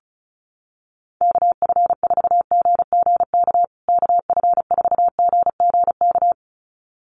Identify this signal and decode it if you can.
Morse code, heard as a keyed tone, about 34 wpm, 700 Hz. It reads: KF4GGK KF4GGK
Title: USA Call Signs Repeated 2x 35wpm